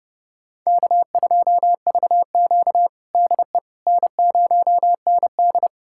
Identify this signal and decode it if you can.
Morse code, heard as a keyed tone, about 32 wpm, 700 Hz. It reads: K2VQ DE N0ND
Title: USA Callsign DE Callsign - 4 Characters - 30wpm